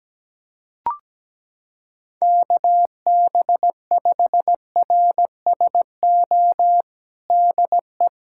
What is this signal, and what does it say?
Morse code, heard as a keyed tone, about 17 wpm, 700 Hz. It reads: KB5RSO DE
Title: USA Callsign DE Callsign - 6 Characters - 17wpm